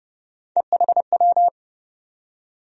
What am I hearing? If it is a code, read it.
Morse code, heard as a keyed tone, about 32 wpm, 700 Hz. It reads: EHW
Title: USA Call Sign Postfixes - 3 Letters 30wpm